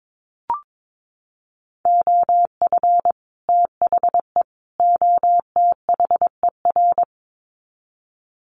Morse code, heard as a keyed tone, about 22 wpm, 700 Hz. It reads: OF THE OTHER
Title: Sets of 3 Words 22wpm